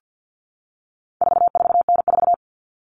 Morse code, heard as a keyed tone, about 50 wpm, 700 Hz. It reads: 44N4
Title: All Cut Numbers - 4-Digit 50wpm